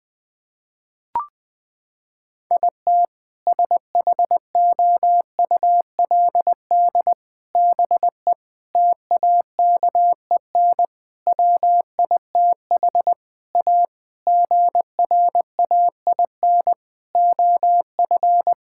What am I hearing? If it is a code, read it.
Morse code, heard as a keyed tone, about 20 wpm, 700 Hz. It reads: IT SHOULD BE TAKEN WITH A GRAIN OF
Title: Sentences with 8-Word Phrases - 20wpm